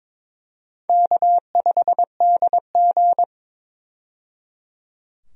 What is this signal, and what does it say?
Morse code, heard as a keyed tone, about 21 wpm, 700 Hz. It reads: K5DG
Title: USA Call Signs - 4 Characters - 22wpm